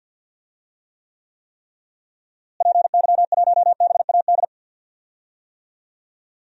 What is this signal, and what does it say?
Morse code, heard as a keyed tone, about 48 wpm, 700 Hz. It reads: PY1BAD
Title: DXCC-100 Call Signs - 6 Characters - Repeated 3x - 50wpm